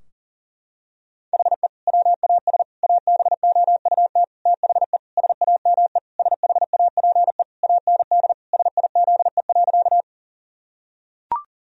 Morse code, heard as a keyed tone, about 40 wpm, 700 Hz. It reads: HE WAS ABOUT THE SAME SHAPE AND SIZE.
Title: Sentences from Top 1000 Words - Review Part 3 - 40wpm